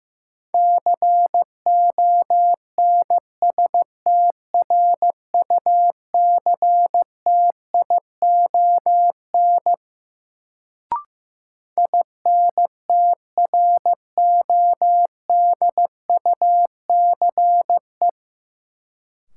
Morse code, heard as a keyed tone, about 15 wpm, 700 Hz. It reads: CONSTRUCTION INTRODUCE
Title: Sentences from Top 1600 Words - New Word Drill - 15wpm